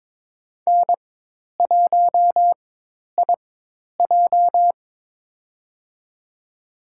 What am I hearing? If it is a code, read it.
Morse code, heard as a keyed tone, about 22 wpm, spaced as for 10 wpm, 700 Hz. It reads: N1IJ
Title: USA Call Signs - 4x Character Spacing - 22wpm